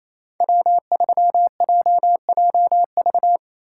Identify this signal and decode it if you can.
Morse code, heard as a keyed tone, about 27 wpm, 700 Hz. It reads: W3JJV
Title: USA Call Signs - 5 Characters - 4x Character Spacing - 28wpm